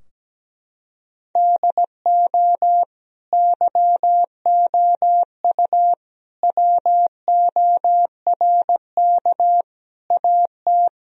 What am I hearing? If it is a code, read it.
Morse code, heard as a keyed tone, about 17 wpm, 700 Hz. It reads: DO YOU WORK AT
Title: Sentences from Top 1000 Words - Review Part 2 - 17wpm